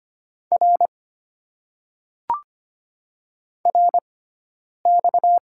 Morse code, heard as a keyed tone, about 25 wpm, spaced as for 9 wpm, 690 Hz. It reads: R RX
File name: Top 90 QSO Elements - 6x Character Spacing - 25wpm